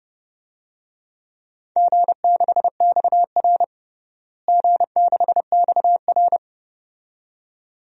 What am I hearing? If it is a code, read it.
Morse code, heard as a keyed tone, about 30 wpm, 700 Hz. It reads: G6XR G6XR
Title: DXCC-100 Call Signs - 4 Characters - Repeated 2x - 30wpm